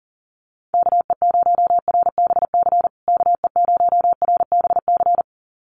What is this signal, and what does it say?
Morse code, heard as a keyed tone, about 40 wpm, 700 Hz. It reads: KE0RBC KE0RBC
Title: USA Call Signs Repeated 2x 40wpm